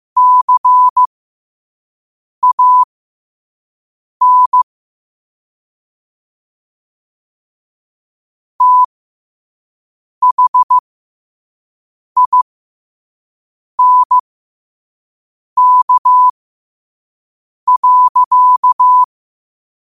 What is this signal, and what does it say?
Morse code, heard as a keyed tone, about 15 wpm, spaced as for 5 wpm, 1000 Hz. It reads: CAN THINK.
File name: qotdfiles / 5wpm